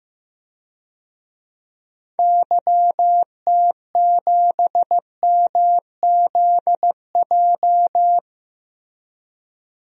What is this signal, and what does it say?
Morse code, heard as a keyed tone, about 15 wpm, 700 Hz. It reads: YT7MZJ
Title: DXCC-100 Call Signs - 6 Characters - 15wpm